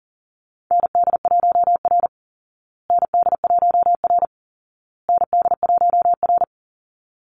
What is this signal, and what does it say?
Morse code, heard as a keyed tone, about 40 wpm, 700 Hz. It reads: ND1R ND1R ND1R
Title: Call Signs Repeated 2x - 40wpm